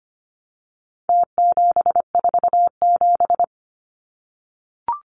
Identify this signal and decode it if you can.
Morse code, heard as a keyed tone, about 25 wpm, 700 Hz. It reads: T747
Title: Numbers - Contest Style - 4-Digit 25wpm